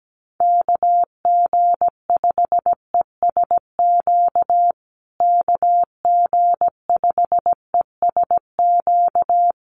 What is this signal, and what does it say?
Morse code, heard as a keyed tone, about 17 wpm, 700 Hz. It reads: KG5ESQ KG5ESQ
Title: USA Call Signs Repeated 3x 17wpm